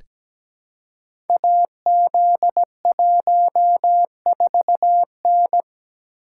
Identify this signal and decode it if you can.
Morse code, heard as a keyed tone, about 17 wpm, 700 Hz. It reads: AZ14N
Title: Code Groups - 5 Letter:Numbers - 17wpm